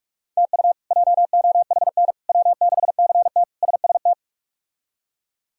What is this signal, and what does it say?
Morse code, heard as a keyed tone, about 46 wpm, 700 Hz. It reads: TU JOHN W6CT SST
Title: K1USN SST QSO Practice - 45wpm